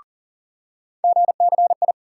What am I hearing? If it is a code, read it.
Morse code, heard as a keyed tone, about 40 wpm, 700 Hz. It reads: GCI
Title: USA Call Sign Postfixes - 3 Letters 40wpm